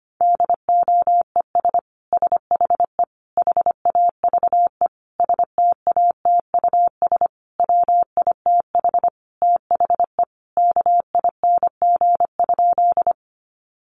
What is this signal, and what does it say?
Morse code, heard as a keyed tone, about 25 wpm, 700 Hz. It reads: DOES SHE HAVE STATUS WITH THE KING?
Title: Sentences from Top 1400 Words 25wpm